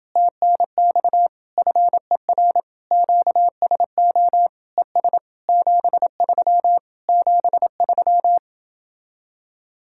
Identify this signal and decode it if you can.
Morse code, heard as a keyed tone, about 28 wpm, 700 Hz. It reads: TNX FER QSO ES 73 73